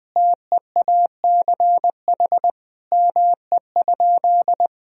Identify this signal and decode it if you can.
Morse code, heard as a keyed tone, about 20 wpm, 700 Hz. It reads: TEACH ME?
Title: Sentences from Top 1000 Words - Review Part 2 - 20wpm